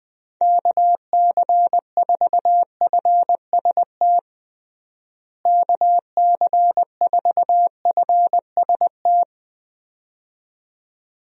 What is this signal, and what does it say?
Morse code, heard as a keyed tone, about 20 wpm, 700 Hz. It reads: KC4FST KC4FST
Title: USA Call Signs - 6 Characters - Repeated 2x - 20wpm